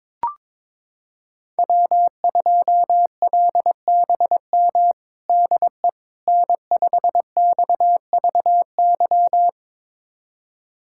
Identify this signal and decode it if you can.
Morse code, heard as a keyed tone, about 22 wpm, 700 Hz. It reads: W2LBM DE N5XVY
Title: USA Callsign DE Callsign 22wpm